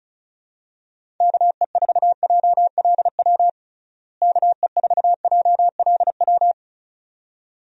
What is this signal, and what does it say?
Morse code, heard as a keyed tone, about 36 wpm, 700 Hz. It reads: KE4JLW KE4JLW
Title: USA Call Signs - 6 Characters - Repeated 2x - 35wpm